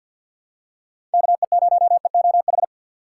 Morse code, heard as a keyed tone, about 46 wpm, 700 Hz. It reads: KE0EOH
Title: USA Call Signs - 6 Characters - 4x Character Spacing - 50wpm